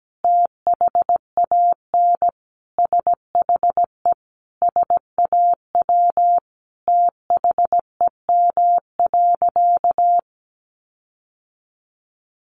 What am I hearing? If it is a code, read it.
Morse code, heard as a keyed tone, about 17 wpm, 700 Hz. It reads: THAN SHE SAW THEM.
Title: Sentences from Top 1300 Words 17wpm